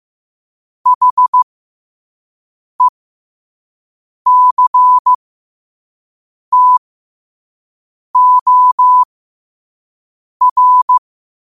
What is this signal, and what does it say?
Morse code, heard as a keyed tone, about 15 wpm, spaced as for 5 wpm, 1000 Hz. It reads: HECTOR
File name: qotdfiles / 5wpm